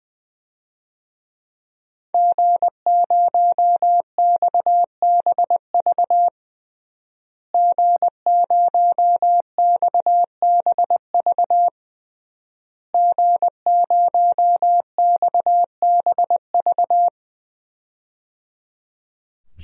Morse code, heard as a keyed tone, about 20 wpm, 700 Hz. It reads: G0XBV G0XBV G0XBV
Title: DXCC-100 Call Signs - 5 Characters - Repeated 3x - 20wpm